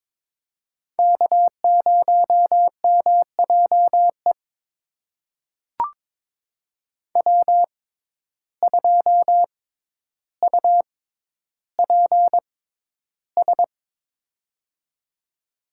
Morse code, heard as a keyed tone, about 22 wpm, spaced as for 11 wpm, 700 Hz. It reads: K0MJE W2UPS
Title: USA Call Signs - 5 Characters - 6x Character Spacing - 22wpm